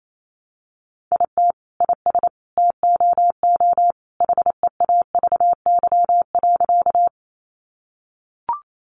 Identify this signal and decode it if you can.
Morse code, heard as a keyed tone, about 28 wpm, 700 Hz. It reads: IT IS TOO HEAVY.
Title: Sentences from Top 1100 Words 28wpm